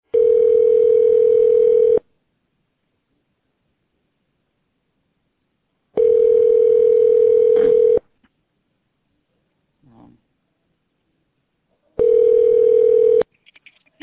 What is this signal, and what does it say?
Ringback tone (North America), 440 + 480 Hz.